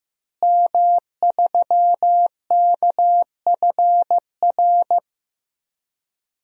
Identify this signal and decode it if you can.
Morse code, heard as a keyed tone, about 15 wpm, 700 Hz. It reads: M3KFR
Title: DXCC-100 Call Signs - 5 Characters - Repeated 2x - 15wpm